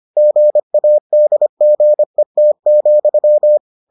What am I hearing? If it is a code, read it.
Morse code, heard as a keyed tone, about 25 wpm, 600 Hz. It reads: GADGET,